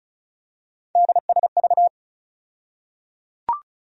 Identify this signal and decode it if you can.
Morse code, heard as a keyed tone, about 36 wpm, 700 Hz. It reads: DSV
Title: USA Call Sign Postfixes - 3 Letters 35wpm